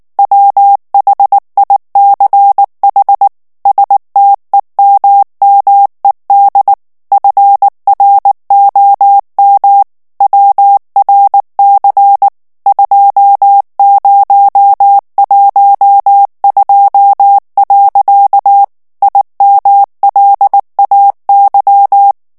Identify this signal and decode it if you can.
Morse code, heard as a keyed tone, about 19 wpm, 800 Hz. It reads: WHICH STEMMED FROM WRC 2012. IMLAY
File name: ANGLAIS texte clair 3 vitesse 18 mots minute